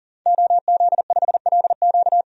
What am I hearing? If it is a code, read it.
Morse code, heard as a keyed tone, about 40 wpm, 700 Hz. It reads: OZ5LQ
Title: DXCC-100 Call Signs - 5 Characters - 40wpm